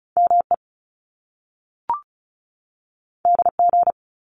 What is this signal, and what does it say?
Morse code, heard as a keyed tone, about 36 wpm, 700 Hz. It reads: ME DG
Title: Top Two Letter Combinations in Words 35wpm